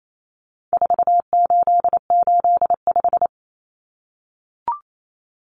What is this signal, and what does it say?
Morse code, heard as a keyed tone, about 28 wpm, 700 Hz. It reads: 4885
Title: Common Cut Numbers - 4-Digit 28wpm